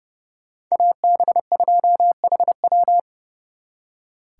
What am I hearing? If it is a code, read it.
Morse code, heard as a keyed tone, about 35 wpm, 700 Hz. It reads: AB2HW
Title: Speed Racing - USA 2x2 Call Signs 30wpm